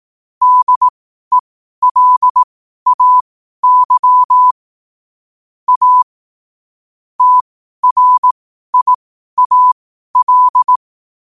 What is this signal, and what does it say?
Morse code, heard as a keyed tone, about 18 wpm, spaced as for 12 wpm, 1000 Hz. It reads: DELAY A TRIAL
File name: qotdfiles / 12wpm